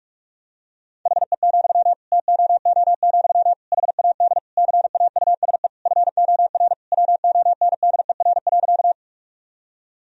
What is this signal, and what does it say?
Morse code, heard as a keyed tone, about 41 wpm, 700 Hz. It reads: HE, TOO, HAD CAUSE FOR WONDER.
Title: Sentences from Top 1000 Words - Review Part 3 - 45wpm